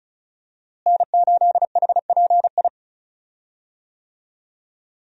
Morse code, heard as a keyed tone, about 36 wpm, 700 Hz. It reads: N8HPI
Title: USA Call Signs - 5 Characters - Repeated 2x - 35wpm